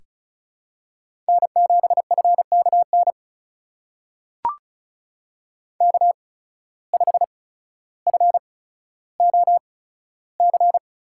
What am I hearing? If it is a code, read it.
Morse code, heard as a keyed tone, about 36 wpm, spaced as for 15 wpm, 700 Hz. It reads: N7FKN K5FOC
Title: USA Call Signs - 5 Characters - 8x Character Spacing - 35wpm